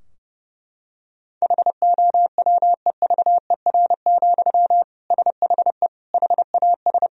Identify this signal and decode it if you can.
Morse code, heard as a keyed tone, about 32 wpm, 700 Hz. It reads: HOWEVER, SHE HAS
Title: Sentences from Top 1000 Words - Review Part 2 - 30wpm